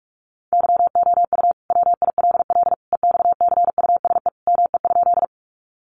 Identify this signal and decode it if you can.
Morse code, heard as a keyed tone, about 44 wpm, 700 Hz. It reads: YOU WILL EXCUSE ME?
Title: Sentences from Top 700 Words 45wpm